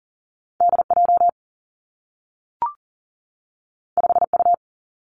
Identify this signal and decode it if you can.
Morse code, heard as a keyed tone, about 40 wpm, 700 Hz. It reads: D J 5 U
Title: Two Challenging Letter-Number Combinations 40wpm